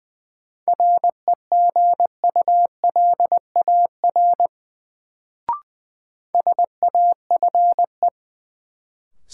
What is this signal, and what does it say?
Morse code, heard as a keyed tone, about 20 wpm, 700 Hz. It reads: REGULAR SAFE
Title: Sentences from Top 900 Words - New Word Drill - 20wpm